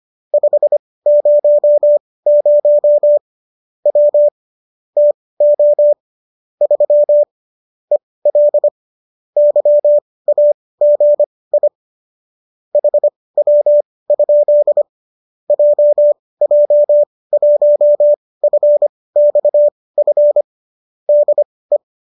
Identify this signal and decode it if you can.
Morse code, heard as a keyed tone, about 25 wpm, spaced as for 18 wpm, 590 Hz. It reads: 500 W TO 3 EL YAGI HW? JJ1FXF DE